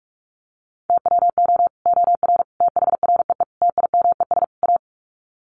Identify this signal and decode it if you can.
Morse code, heard as a keyed tone, about 44 wpm, 700 Hz. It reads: TWO OR THREE TIMES A
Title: Sets of 5 Words Encore2 45wpm